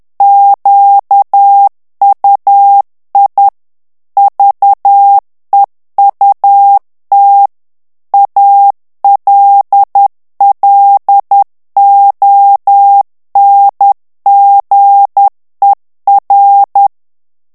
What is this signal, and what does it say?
Morse code, heard as a keyed tone, about 10 wpm, 800 Hz. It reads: QUI VEUT ALLONGER
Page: Français - texte en clair 1 - vitesse : 10 mots minute